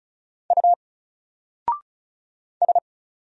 Morse code, heard as a keyed tone, about 33 wpm, 700 Hz. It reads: U S
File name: Single-Letter-Number-35wpm